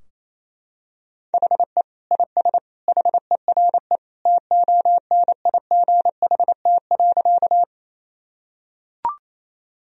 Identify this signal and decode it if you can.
Morse code, heard as a keyed tone, about 28 wpm, 700 Hz. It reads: HE IS HERE TONIGHT.
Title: Sentences from Top 1000 Words - Review Part 3 - 28wpm